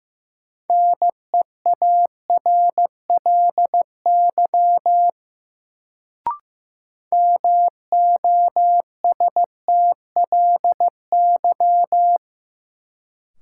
Morse code, heard as a keyed tone, about 15 wpm, 700 Hz. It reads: NEARLY MOSTLY
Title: Sentences from Top 1300 Words - New Word Drill Alt - 15wpm